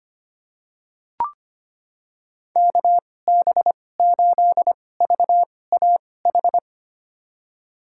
Morse code, heard as a keyed tone, about 25 wpm, spaced as for 18 wpm, 700 Hz. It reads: KB8VAH
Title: USA Call Signs - 2x Character Spacing - 25wpm